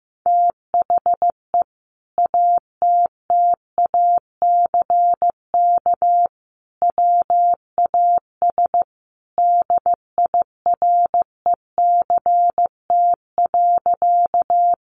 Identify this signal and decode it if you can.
Morse code, heard as a keyed tone, about 14 wpm, 700 Hz. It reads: THE ATTACK WAS DIRECT.
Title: Sentences from Top 1500 Words 15wpm